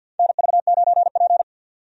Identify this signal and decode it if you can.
Morse code, heard as a keyed tone, about 52 wpm, 700 Hz. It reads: NV9P
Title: USA Call Signs - 4 Characters - 50wpm